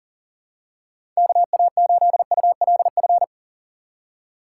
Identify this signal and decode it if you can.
Morse code, heard as a keyed tone, about 40 wpm, 700 Hz. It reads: KA8ULF
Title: USA Call Signs - 6x Character Spacing - 40wpm